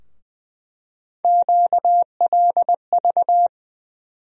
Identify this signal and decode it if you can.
Morse code, heard as a keyed tone, about 20 wpm, 700 Hz. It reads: QLV
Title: USA Call Sign Postfixes - 3 Letters 20wpm